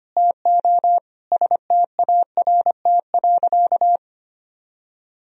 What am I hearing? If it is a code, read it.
Morse code, heard as a keyed tone, about 25 wpm, 700 Hz. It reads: TO START.
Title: Sentences from Top 1000 Words - Review Part 1 - 25wpm